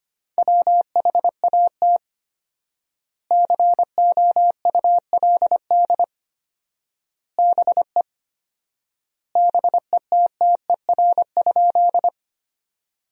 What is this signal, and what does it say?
Morse code, heard as a keyed tone, about 25 wpm, 700 Hz. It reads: WHAT COULD BE BETTER?
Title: 4x Word-Spacing - Sentences from Top 400 Words - 25wpm